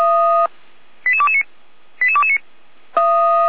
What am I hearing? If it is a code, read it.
Five-tone selective call (ZVEI), heard as quick runs of five tones, tones of about 70 ms. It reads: 88229 88229